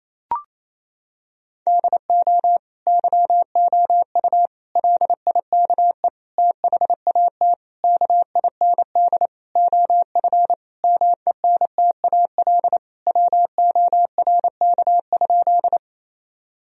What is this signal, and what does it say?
Morse code, heard as a keyed tone, about 28 wpm, 700 Hz. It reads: DO YOU LIKE THAT KIND OF MENTAL WORK?
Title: Sentences from Top 1000 Words - Review Part 4 - 28wpm